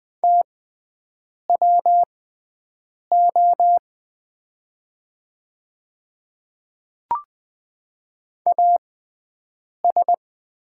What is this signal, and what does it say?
Morse code, heard as a keyed tone, about 20 wpm, spaced as for 7 wpm, 700 Hz. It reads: TWO AS
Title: 6x Character Spacing - Top 100 Words - 20wpm